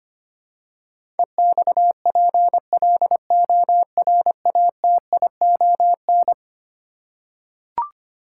Morse code, heard as a keyed tone, about 25 wpm, 700 Hz. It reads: EXPLORATION
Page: morse-code-ninja-practice-25wpm